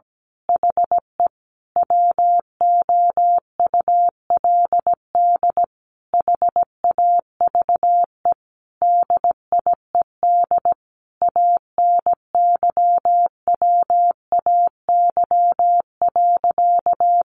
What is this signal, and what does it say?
Morse code, heard as a keyed tone, about 17 wpm, 700 Hz. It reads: HE WOULD HAVE DIED ANYWAY.
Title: A Sound Of Thunder 17wpm